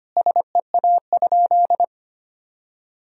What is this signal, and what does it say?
Morse code, heard as a keyed tone, about 25 wpm, 700 Hz. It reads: SEA?
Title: Sentences from Top 1000 Words - Review Part 1 - 25wpm